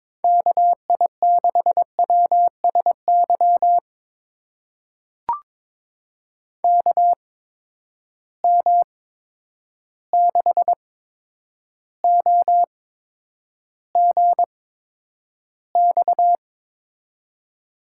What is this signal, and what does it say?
Morse code, heard as a keyed tone, about 22 wpm, spaced as for 9 wpm, 700 Hz. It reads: KI6WSY KM6OGX
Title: USA Call Signs - 6 Characters - 8x Character Spacing - 22wpm